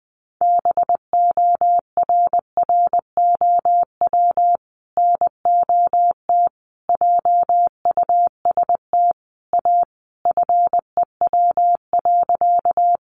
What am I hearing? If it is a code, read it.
Morse code, heard as a keyed tone, about 20 wpm, 700 Hz. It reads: BORROW NOT JUST A FEW.
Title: Sentences from Top 1500 Words 20wpm